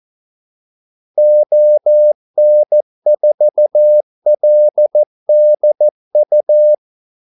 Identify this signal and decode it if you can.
Morse code, heard as a keyed tone, about 14 wpm, 600 Hz. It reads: ON4LDU
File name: Callsigns-14wpm